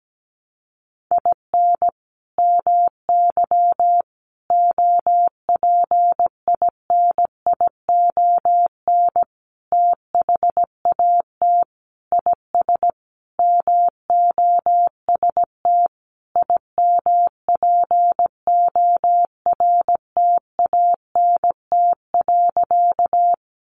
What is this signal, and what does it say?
Morse code, heard as a keyed tone, about 17 wpm, 700 Hz. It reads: IN MY OPINION THAT IS MOST IMPORTANT.
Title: Sentences from Top 1200 Words 17wpm